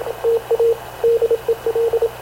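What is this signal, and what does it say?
Morse code, heard as a keyed tone, about 26 wpm, 460 Hz, with radio noise. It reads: TA DEL
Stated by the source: Havana Radio CLA press (PX) broadcast, recorded by Radio Officer Paul Dumesnil aboard mv CNAV QUEST/CZDO.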